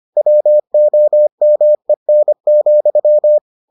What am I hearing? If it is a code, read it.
Morse code, heard as a keyed tone, about 25 wpm, 600 Hz. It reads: WOMEN,